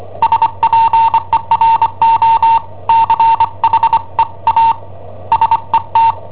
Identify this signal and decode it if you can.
Morse code, heard as a keyed tone, about 25 wpm, 910 Hz. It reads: SPERO CHE A SET
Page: Cassetta esercizi preparata da I7OHP per IW5ELP